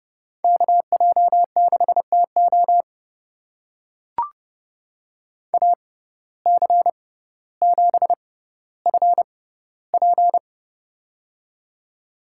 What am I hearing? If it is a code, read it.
Morse code, heard as a keyed tone, about 30 wpm, spaced as for 15 wpm, 700 Hz. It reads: KJ6TO AC7FP
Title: USA Call Signs - 5 Characters - 6x Character Spacing - 30wpm